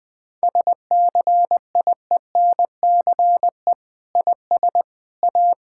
Morse code, heard as a keyed tone, about 20 wpm, 700 Hz. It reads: SCIENCE IS A
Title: Sentences from Top 1000 Words - Review Part 6 - 20wpm